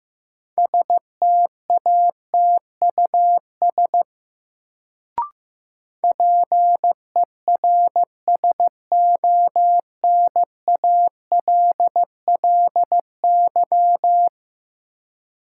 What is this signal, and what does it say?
Morse code, heard as a keyed tone, about 15 wpm, 700 Hz. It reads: STATUS PERSONALLY
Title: Sentences from Top 1400 Words - New Word Drill - 15wpm